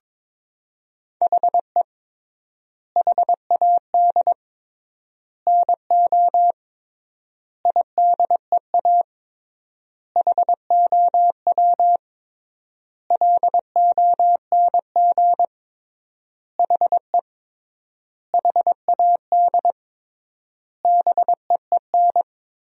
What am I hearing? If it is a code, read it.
Morse code, heard as a keyed tone, about 22 wpm, 700 Hz. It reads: HE HAD NO IDEA HOW LONG HE HAD BEEN
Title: 3x Word-Spacing - Sentences from Top 300 Words - 22wpm